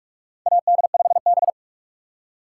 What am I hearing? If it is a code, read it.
Morse code, heard as a keyed tone, about 44 wpm, 690 Hz. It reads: AD5B
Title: USA Call Signs - 4 Characters - 2x Character Spacing - 45wpm